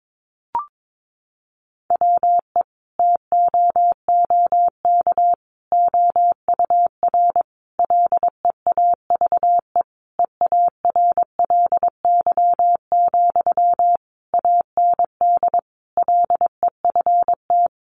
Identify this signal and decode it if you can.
Morse code, heard as a keyed tone, about 22 wpm, 700 Hz. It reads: WE TOOK OUR LEAVE EARLY, AND LEFT
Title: Sentences from Top 400 Words 22wpm